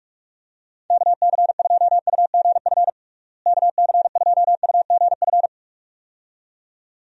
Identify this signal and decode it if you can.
Morse code, heard as a keyed tone, about 42 wpm, 700 Hz. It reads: KC2UGF KC2UGF
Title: USA Call Signs - 6 Characters - Repeated 2x - 45wpm